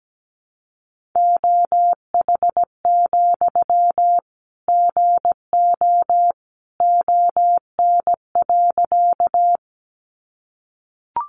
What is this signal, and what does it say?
Morse code, heard as a keyed tone, about 17 wpm, 700 Hz. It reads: OH, GO ON.
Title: Sentences from Top 500 Words 17wpm